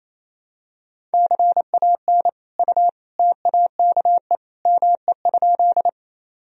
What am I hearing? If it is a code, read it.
Morse code, heard as a keyed tone, about 28 wpm, 700 Hz. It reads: CAN U TAKE ME?
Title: Sentences from Top 200 Words and Abbreviations 28wpm